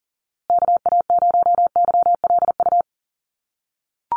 Morse code, heard as a keyed tone, about 40 wpm, 700 Hz. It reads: KA0YLU
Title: USA Call Signs Repeated 2x 40wpm